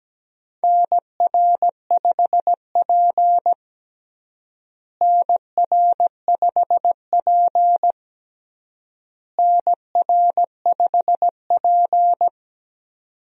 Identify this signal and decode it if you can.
Morse code, heard as a keyed tone, about 17 wpm, 700 Hz. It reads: NR5P NR5P NR5P
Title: USA Call Signs - 4 Characters - Repeated 3x - 17wpm